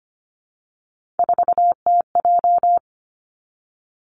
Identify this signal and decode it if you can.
Morse code, heard as a keyed tone, about 25 wpm, 700 Hz. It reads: 4TJ
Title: Three Challenging Letter-Number Combinations 25wpm